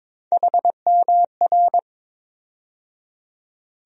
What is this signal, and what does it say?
Morse code, heard as a keyed tone, about 22 wpm, 700 Hz. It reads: HMR
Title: USA Call Sign Postfixes - 3 Letters 22wpm